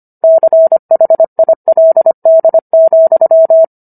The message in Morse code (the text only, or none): CHILD,